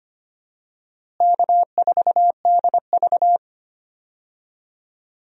Morse code, heard as a keyed tone, about 25 wpm, 700 Hz. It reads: K4DV
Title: USA Call Signs - 4 Characters - Repeated 2x - 25wpm